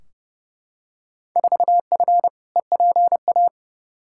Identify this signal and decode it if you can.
Morse code, heard as a keyed tone, about 30 wpm, 700 Hz. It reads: 4F EPA
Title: Field Day Exchanges - 30wpm